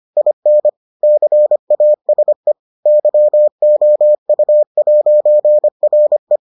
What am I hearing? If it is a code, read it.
Morse code, heard as a keyed tone, about 25 wpm, 600 Hz. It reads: IN CASE YOU'RE